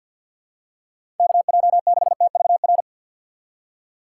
Morse code, heard as a keyed tone, about 53 wpm, 700 Hz. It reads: KJ6TVR
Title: USA Call Signs - 6 Characters - 3x Character Spacing - 50wpm